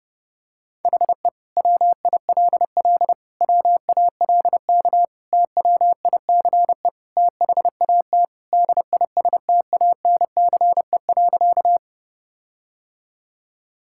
Morse code, heard as a keyed tone, about 32 wpm, 700 Hz. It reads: HE WILL WALK TWICE THAT DISTANCE.
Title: Sentences from Top 1600 Words 30wpm